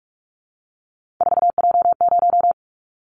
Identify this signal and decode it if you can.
Morse code, heard as a keyed tone, about 46 wpm, 700 Hz. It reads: 4J0
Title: Callsign Prefixes for Top 100 DXCC - 45wpm